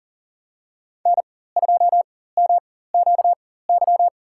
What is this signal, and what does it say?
Morse code, heard as a keyed tone, about 40 wpm, spaced as for 19 wpm, 700 Hz. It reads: N2MQY
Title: USA Call Signs - 5 Characters - 4x Character Spacing - 40wpm